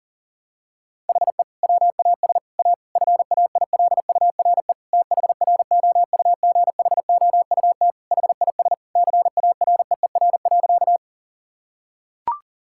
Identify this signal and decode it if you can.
Morse code, heard as a keyed tone, about 40 wpm, 700 Hz. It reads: HE WAS A FAILURE THROUGHOUT HIS CAREER.
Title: Sentences from Top 1000 Words - Review Part 6 - 40wpm